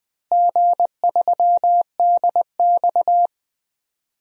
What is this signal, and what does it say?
Morse code, heard as a keyed tone, about 20 wpm, 700 Hz. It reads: G3DX
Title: DXCC-100 Call Signs - 4 Characters - Repeated 2x - 20wpm